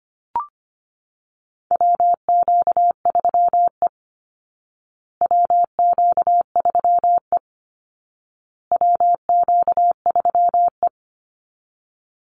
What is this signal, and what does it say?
Morse code, heard as a keyed tone, about 25 wpm, 700 Hz. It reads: WQ3E WQ3E WQ3E
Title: Call Signs Repeated 2x - 25wpm